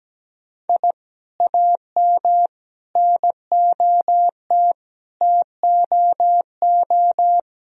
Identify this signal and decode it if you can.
Morse code, heard as a keyed tone, about 17 wpm, 700 Hz. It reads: I AM NOT TOO
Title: Sentences from Top 1000 Words - Review Part 3 - 17wpm